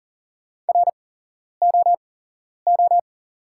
Morse code, heard as a keyed tone, about 42 wpm, spaced as for 11 wpm, 690 Hz. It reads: ROO
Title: 8x Character Spacing - Top 300 Words - 40wpm